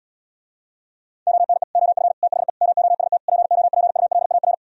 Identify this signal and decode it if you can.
Morse code, heard as a keyed tone, about 78 wpm, 700 Hz. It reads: ONE OF THE MOST POPULAR